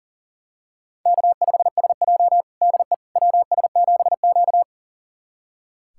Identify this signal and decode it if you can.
Morse code, heard as a keyed tone, about 40 wpm, 700 Hz. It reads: K5SJ DE WS7Q
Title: USA Callsign DE Callsign - 4 Characters - 40wpm